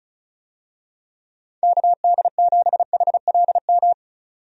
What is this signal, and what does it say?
Morse code, heard as a keyed tone, about 36 wpm, 700 Hz. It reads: KD7HLM
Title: USA Call Signs - 6 Characters - 35wpm